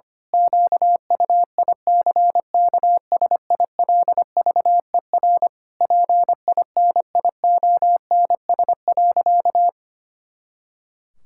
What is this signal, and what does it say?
Morse code, heard as a keyed tone, about 25 wpm, 700 Hz. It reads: QUICKSILVER PINIONS.
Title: Pangram Challenge v2 - 25wpm